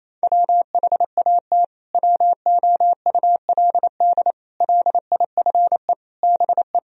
Morse code, heard as a keyed tone, about 28 wpm, 700 Hz. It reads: WHAT WOULD LIFE BE
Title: Sentences from Top 1000 Words - Review Part 3 - 28wpm